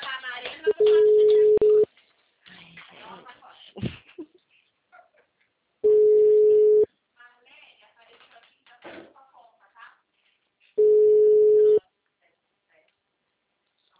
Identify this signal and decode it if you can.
Ringback tone (Europe, 425 Hz), with speech.
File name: out- -2016- - - .62567